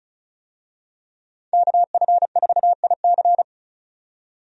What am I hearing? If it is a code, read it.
Morse code, heard as a keyed tone, about 36 wpm, 700 Hz. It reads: KF4IC
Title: USA Call Signs - 5 Characters - 35wpm